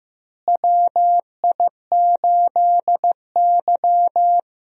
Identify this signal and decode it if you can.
Morse code, heard as a keyed tone, about 15 wpm, 700 Hz. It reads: WI8Y
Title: USA Call Signs - 4 Characters - Repeated 3x - 15wpm